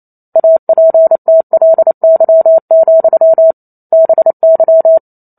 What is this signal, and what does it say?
Morse code, heard as a keyed tone, about 28 wpm, 650 Hz. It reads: APTLY, BY
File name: Chapter 2 - 28 WPM